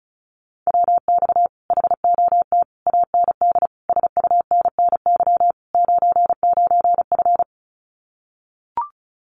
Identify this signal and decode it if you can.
Morse code, heard as a keyed tone, about 34 wpm, 700 Hz. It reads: WX HOT AND SUNNY 99F
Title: Common QSO Phrases Encore 35wpm